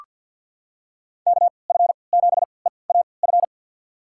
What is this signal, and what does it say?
Morse code, heard as a keyed tone, about 47 wpm, spaced as for 28 wpm, 700 Hz. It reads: KF7EAF
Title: USA Call Signs - 3x Character Spacing - 50wpm